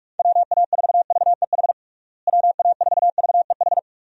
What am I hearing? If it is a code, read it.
Morse code, heard as a keyed tone, about 46 wpm, 700 Hz. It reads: WA4VEH WA4VEH
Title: USA Call Signs - 6 Characters - Repeated 2x - 45wpm